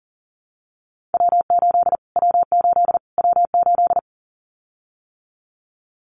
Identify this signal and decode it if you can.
Morse code, heard as a keyed tone, about 40 wpm, 700 Hz. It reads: W8 W8 W8
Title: Callsign Prefixes for Top 25 DXCC - Repeated 3x - 40wpm